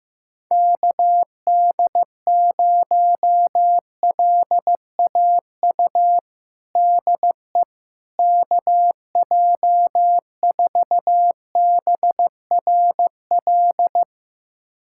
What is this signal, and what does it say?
Morse code, heard as a keyed tone, about 15 wpm, 700 Hz. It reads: KD0LAU DE KJ4BRL
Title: USA Callsign DE Callsign 15wpm